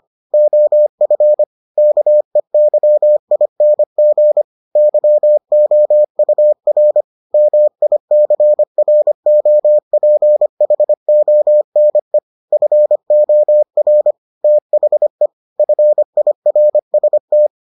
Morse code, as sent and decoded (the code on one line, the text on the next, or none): --- ..-. / -.- . -.-- .. -. --. / -.-- --- ..- .-. / -- .. -.-. .-. --- .--. .... --- -. / ..-. --- .-. / - .... / ..-. .. .-. ... -
OF KEYING YOUR MICROPHONE FOR THE FIRST